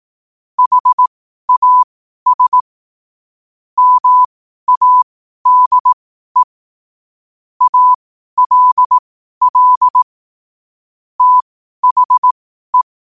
Morse code, heard as a keyed tone, about 18 wpm, spaced as for 12 wpm, 1000 Hz. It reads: HAS MADE ALL THE
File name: qotdfiles / 12wpm